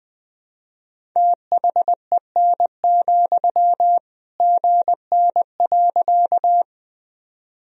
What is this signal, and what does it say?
Morse code, heard as a keyed tone, about 20 wpm, 700 Hz. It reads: THEN, GN.
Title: Sentences from Top 300 Words and Abbreviations 20wpm